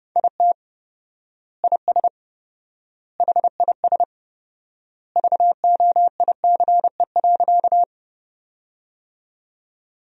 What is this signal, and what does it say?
Morse code, heard as a keyed tone, about 32 wpm, 700 Hz. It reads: IT IS HIS VOICE.
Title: 4x Word-Spacing - Sentences from Top 500 Words - 30wpm